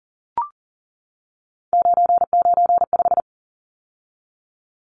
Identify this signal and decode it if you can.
Morse code, heard as a keyed tone, about 40 wpm, 700 Hz. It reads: 995
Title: Numbers - 3-Digit 40wpm